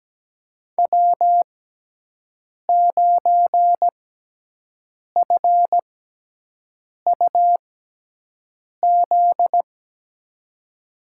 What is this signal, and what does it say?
Morse code, heard as a keyed tone, about 17 wpm, spaced as for 6 wpm, 700 Hz. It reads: W9FUZ
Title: USA Call Signs - 6x Character Spacing - 17wpm